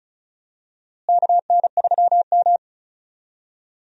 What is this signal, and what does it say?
Morse code, heard as a keyed tone, about 34 wpm, 700 Hz. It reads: KN3M
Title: USA Call Signs - 4 Characters - Repeated 2x - 35wpm